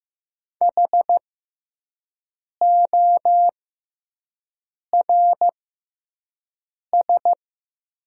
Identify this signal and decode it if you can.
Morse code, heard as a keyed tone, about 15 wpm, spaced as for 5 wpm, 700 Hz. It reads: HORS
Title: 6x Character Spacing - Top 300 Words - 15wpm